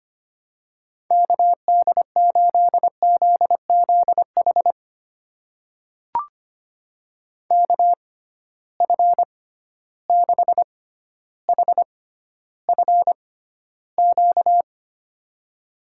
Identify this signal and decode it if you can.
Morse code, heard as a keyed tone, about 25 wpm, spaced as for 13 wpm, 700 Hz. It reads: KD8ZZH KF6HFQ
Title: USA Call Signs - 6 Characters - 6x Character Spacing - 25wpm